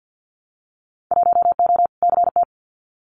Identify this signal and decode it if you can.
Morse code, heard as a keyed tone, about 50 wpm, 700 Hz. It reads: 1O CT
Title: Winter Field Day Exchanges 50wpm